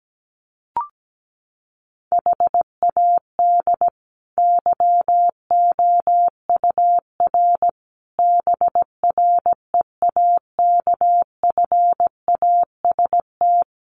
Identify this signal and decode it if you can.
Morse code, heard as a keyed tone, about 17 wpm, 700 Hz. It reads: HAD YOUR BREAKFAST
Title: Sentences from Top 1500 Words 17wpm